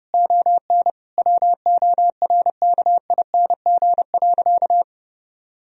Morse code, heard as a keyed tone, about 30 wpm, 700 Hz. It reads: ON WORKING.
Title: Sentences from Top 1000 Words - Review Part 4 - 30wpm